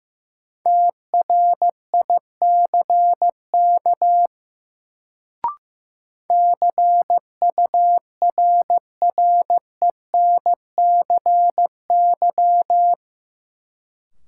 Morse code, heard as a keyed tone, about 15 wpm, 700 Hz. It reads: TRICK CURRENCY
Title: Sentences from Top 1500 Words - New Word Drill - 15wpm